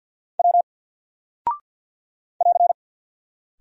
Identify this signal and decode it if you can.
Morse code, heard as a keyed tone, about 48 wpm, 700 Hz. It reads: W <AR>
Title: Single Character (Letter:Number:Punctuation:Prosign) Alt 50wpm